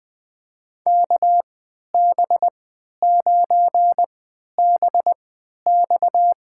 Morse code, heard as a keyed tone, about 20 wpm, spaced as for 11 wpm, 700 Hz. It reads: KB9BX
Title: USA Call Signs - 3x Character Spacing - 20wpm